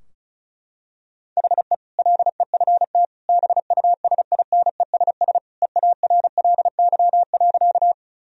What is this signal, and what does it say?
Morse code, heard as a keyed tone, about 36 wpm, 700 Hz. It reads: HE LEFT BUSINESS EARLY.
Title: Sentences from Top 1000 Words - Review Part 2 - 35wpm